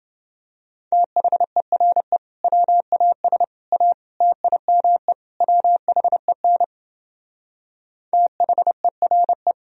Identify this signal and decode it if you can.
Morse code, heard as a keyed tone, about 30 wpm, 700 Hz. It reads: THERE WAS A TIME WHEN THERE